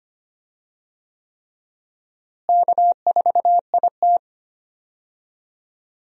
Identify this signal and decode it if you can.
Morse code, heard as a keyed tone, about 25 wpm, 700 Hz. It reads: K4IT
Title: USA Call Signs - 4 Characters - Repeated 3x - 25wpm